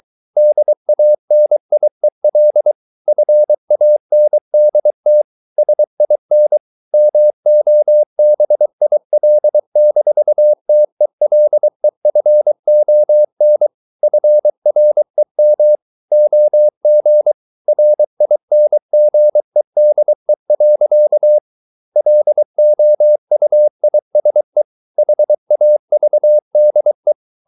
Morse code, heard as a keyed tone, about 23 wpm, 600 Hz. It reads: DANIEL FANDT SIN MOBIL-TELEFON FREM OG RINGEDE. LOUISE HAVDE
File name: Ferie QRS 23wpm